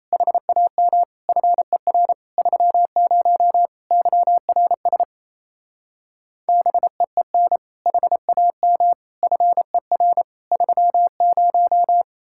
Morse code, heard as a keyed tone, about 36 wpm, 700 Hz. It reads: HAM FER 30 YRS BEEN HAM FER 30